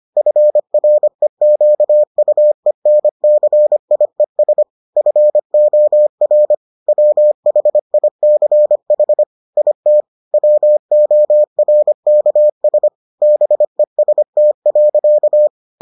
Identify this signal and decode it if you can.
Morse code, heard as a keyed tone, about 25 wpm, 600 Hz. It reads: FREQUENCIES FOR WHICH IT WORKS BEST.